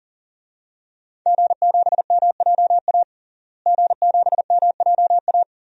Morse code, heard as a keyed tone, about 40 wpm, 700 Hz. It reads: G7MJA G7MJA
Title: DXCC-100 Call Signs - 5 Characters - Repeated 2x - 40wpm